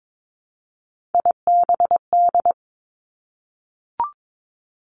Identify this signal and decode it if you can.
Morse code, heard as a keyed tone, about 22 wpm, 700 Hz. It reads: IBD
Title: Three Challenging Letter-Number Combinations 22wpm